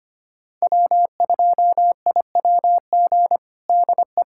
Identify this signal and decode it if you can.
Morse code, heard as a keyed tone, about 25 wpm, 700 Hz. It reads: W2IWG DE
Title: USA Callsign DE Callsign - 5 Characters - 25wpm